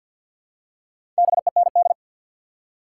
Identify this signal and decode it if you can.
Morse code, heard as a keyed tone, about 47 wpm, 700 Hz. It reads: BEND
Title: Four Letter Words - 50wpm